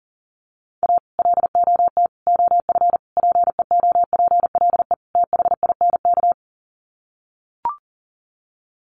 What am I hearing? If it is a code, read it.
Morse code, heard as a keyed tone, about 37 wpm, 700 Hz. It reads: A LOT OF PEOPLE THINK
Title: Sets of 5 Words Encore 40wpm